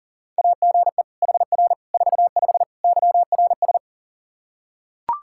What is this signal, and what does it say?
Morse code, heard as a keyed tone, about 40 wpm, 700 Hz. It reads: AGE HR 45 YRS
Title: K1USN SST QSO Practice 40wpm